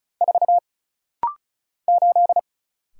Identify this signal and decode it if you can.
Morse code, heard as a keyed tone, about 35 wpm, 700 Hz. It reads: 4 8
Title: Single Letter-Number - Rapid-Fire 35wpm